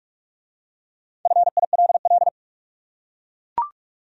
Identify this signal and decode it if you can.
Morse code, heard as a keyed tone, about 43 wpm, 700 Hz. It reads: FILL